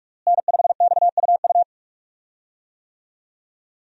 Morse code, heard as a keyed tone, about 46 wpm, 700 Hz. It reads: N5XUU
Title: USA Call Signs - 5 Characters - Repeated 2x - 45wpm